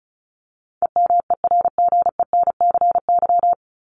Morse code, heard as a keyed tone, about 34 wpm, 700 Hz. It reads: EMERGENCY
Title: Top 1501-1750 Words 35wpm